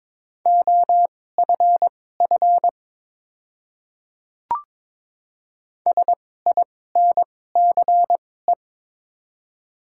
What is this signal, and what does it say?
Morse code, heard as a keyed tone, about 22 wpm, spaced as for 16 wpm, 700 Hz. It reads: OFF SINCE
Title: 2x Character Spacing - Top 500 Words - 22wpm